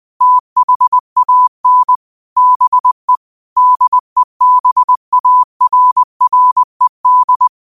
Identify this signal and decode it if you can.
Morse code, heard as a keyed tone, about 20 wpm, 1000 Hz. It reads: THAN BE DEBARRED
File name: qotdfiles / 20wpm